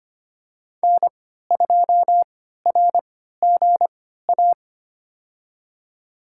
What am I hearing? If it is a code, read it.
Morse code, heard as a keyed tone, about 25 wpm, spaced as for 14 wpm, 700 Hz. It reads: N2RGA
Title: USA Call Signs - 5 Characters - 3x Character Spacing - 25wpm